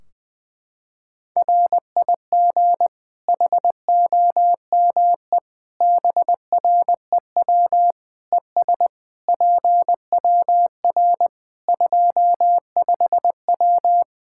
Morse code, heard as a keyed tone, about 20 wpm, 700 Hz. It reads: RIG HOME BREW ES PWR 25W
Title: Practice QSO - Exchange 2 - Info - 20wpm